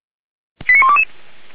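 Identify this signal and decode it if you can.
Five-tone selective call (ZVEI), heard as quick runs of five tones, tones of about 70 ms. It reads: 89122
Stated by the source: ZVEI five-tone selective call received on an SDR.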